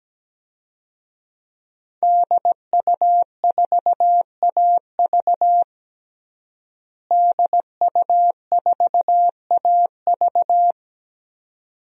Morse code, heard as a keyed tone, about 17 wpm, 700 Hz. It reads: DU4AV DU4AV
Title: DXCC-100 Call Signs - 5 Characters - Repeated 2x - 17wpm